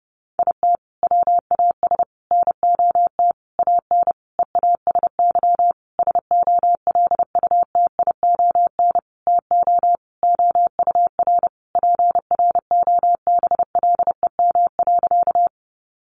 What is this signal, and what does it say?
Morse code, heard as a keyed tone, about 30 wpm, 700 Hz. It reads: IT WAS NOT AN EASY SOLUTION TO OUR PROBLEM.
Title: Sentences from Top 1300 Words 30wpm